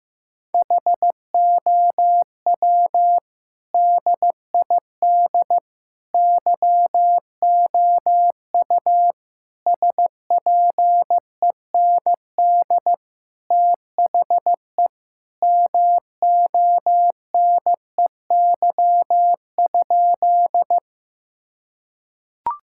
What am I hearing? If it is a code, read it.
Morse code, heard as a keyed tone, about 15 wpm, 700 Hz. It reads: HOW DID YOU SPEND THE MONEY?
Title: Sentences from Top 1000 Words - Review Part 5 - 15wpm